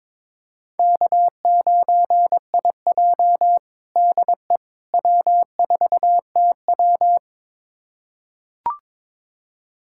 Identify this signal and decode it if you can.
Morse code, heard as a keyed tone, about 22 wpm, 700 Hz. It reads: K9IJ DE W4TW
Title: USA Callsign DE Callsign - 4 Characters - 22wpm